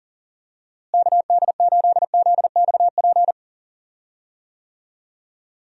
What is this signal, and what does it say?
Morse code, heard as a keyed tone, about 40 wpm, 700 Hz. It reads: KD8ZXP
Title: USA Call Signs - 6 Characters - Repeated 3x - 40wpm